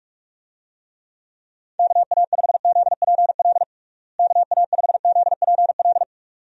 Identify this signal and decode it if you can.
Morse code, heard as a keyed tone, about 46 wpm, 700 Hz. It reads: KA5ZPL KA5ZPL
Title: USA Call Signs - 6 Characters - Repeated 2x - 45wpm